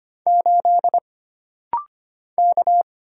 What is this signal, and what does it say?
Morse code, heard as a keyed tone, about 25 wpm, 700 Hz. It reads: 8 K
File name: Single Character (Letter:Number:Punctuation:Prosign) - Rapid-Fire - 25wpm